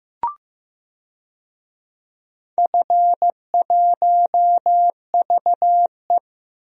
Morse code, heard as a keyed tone, about 15 wpm, 700 Hz. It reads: F1VE
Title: DXCC-100 Call Signs - 4 Characters - 15wpm